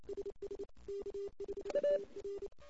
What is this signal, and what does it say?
Morse code, heard as a keyed tone, about 29 wpm, 400 Hz. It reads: SS K5TR
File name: ss-cw